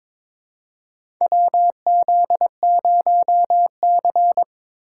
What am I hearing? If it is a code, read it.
Morse code, heard as a keyed tone, about 21 wpm, 700 Hz. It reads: WZ0C
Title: USA Call Signs - 4 Characters - 3x Character Spacing - 22wpm